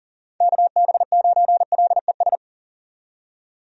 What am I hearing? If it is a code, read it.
Morse code, heard as a keyed tone, about 40 wpm, 700 Hz. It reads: KB9LES
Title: USA Call Signs - 6 Characters - 40wpm